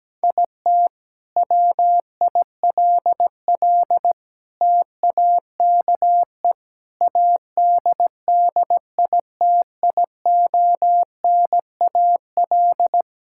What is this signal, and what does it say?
Morse code, heard as a keyed tone, about 17 wpm, 700 Hz. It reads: IT WILL TAKE ADDITIONAL
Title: Sentences from Top 1000 Words - Review Part 5 - 17wpm